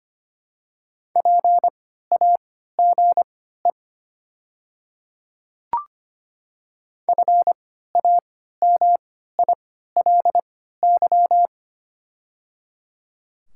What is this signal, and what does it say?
Morse code, heard as a keyed tone, about 25 wpm, spaced as for 14 wpm, 700 Hz. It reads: PAGE FAMILY
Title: 3x Character Spacing - Top 500 Words - 25wpm